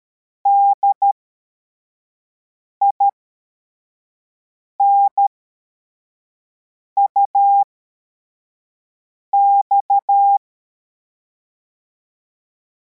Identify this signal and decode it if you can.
Morse code, heard as a keyed tone, about 13 wpm, spaced as for 4 wpm, 800 Hz. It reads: DINUX
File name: formation / cw / Text43